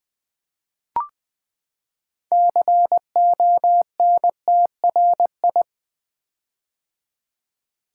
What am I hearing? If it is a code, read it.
Morse code, heard as a keyed tone, about 20 wpm, 700 Hz. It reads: CONTRI
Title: Build A Word - 6 Letters At A Time - 12-Letter Words - 20wpm